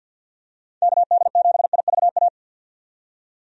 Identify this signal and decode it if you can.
Morse code, heard as a keyed tone, about 50 wpm, 700 Hz. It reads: KD7IVA
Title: USA Call Signs - 6 Characters - 50wpm